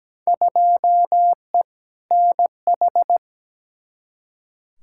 Morse code, heard as a keyed tone, about 17 wpm, 700 Hz. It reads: 2E NH
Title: Field Day Exchanges - 17wpm